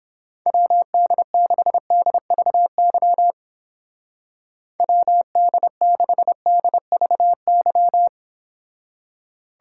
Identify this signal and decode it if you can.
Morse code, heard as a keyed tone, about 31 wpm, 700 Hz. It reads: WD6DVY WD6DVY